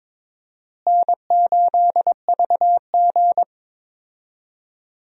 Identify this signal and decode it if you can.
Morse code, heard as a keyed tone, about 21 wpm, 700 Hz. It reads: N8VG
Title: USA Call Signs - 4 Characters - Repeated 2x - 22wpm